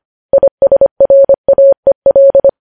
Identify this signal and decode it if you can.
Morse code, heard as a keyed tone, about 25 wpm, 550 Hz. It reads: ISRAEL
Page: news-headlines-in-morse-code-at-25-wpm